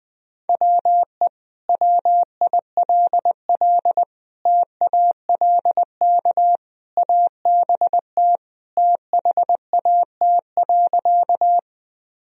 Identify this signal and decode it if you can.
Morse code, heard as a keyed tone, about 20 wpm, 700 Hz. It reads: WE WILL TALK ABT THAT.
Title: Sentences from Top 400 Words and Abbreviations 20wpm